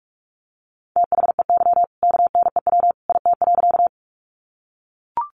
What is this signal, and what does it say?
Morse code, heard as a keyed tone, about 44 wpm, 700 Hz. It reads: THEY KNEW IT.
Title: Sentences from Top 400 Words 45wpm